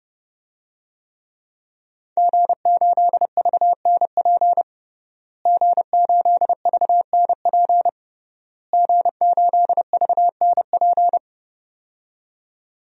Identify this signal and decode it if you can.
Morse code, heard as a keyed tone, about 32 wpm, 700 Hz. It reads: G8VNP G8VNP G8VNP
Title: DXCC-100 Call Signs - 5 Characters - Repeated 3x - 30wpm